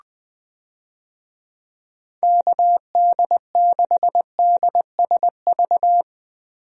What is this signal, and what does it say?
Morse code, heard as a keyed tone, about 20 wpm, 700 Hz. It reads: KD6DSV
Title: USA Call Signs - 6 Characters - 20wpm